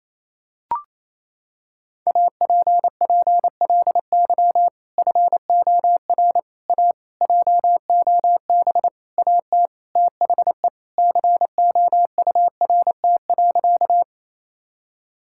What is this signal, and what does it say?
Morse code, heard as a keyed tone, about 28 wpm, 700 Hz. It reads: APPLY FOR A JOB AT THE COURT.
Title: Sentences from Top 1600 Words 28wpm